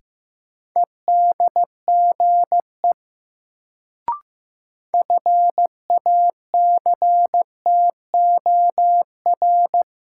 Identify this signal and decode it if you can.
Morse code, heard as a keyed tone, about 15 wpm, 700 Hz. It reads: EDGE FACTOR
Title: Sentences from Top 1300 Words - New Word Drill - 15wpm